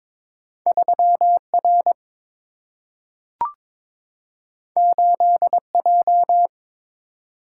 Morse code, heard as a keyed tone, about 22 wpm, 700 Hz. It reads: 3R 8J
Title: Single-Letter Single-Number 22wpm